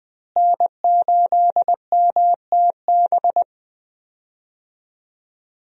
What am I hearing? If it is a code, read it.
Morse code, heard as a keyed tone, about 20 wpm, 700 Hz. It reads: N8MTB
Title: USA Call Signs - 5 Characters - Repeated 3x - 20wpm